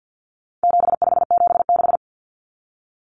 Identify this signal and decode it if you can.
Morse code, heard as a keyed tone, about 50 wpm, 700 Hz. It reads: 7576
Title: Common Cut Numbers - 4-Digit 50wpm